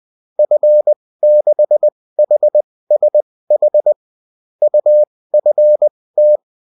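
Morse code, heard as a keyed tone, about 20 wpm, spaced as for 16 wpm, 600 Hz. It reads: F6HSH UFT
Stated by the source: Morse code bulletin of a French amateur CW union, sent as text and in CW.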